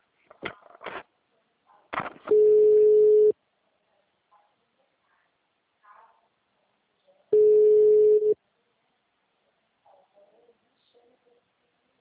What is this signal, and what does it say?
Ringback tone (Europe, 425 Hz).
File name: out- -2036- - - .40414